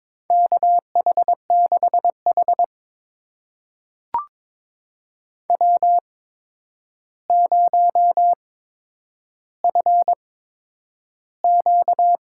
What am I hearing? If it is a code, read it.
Morse code, heard as a keyed tone, about 22 wpm, spaced as for 9 wpm, 700 Hz. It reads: KH6H W0FQ
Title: USA Call Signs - 4 Characters - 8x Character Spacing - 22wpm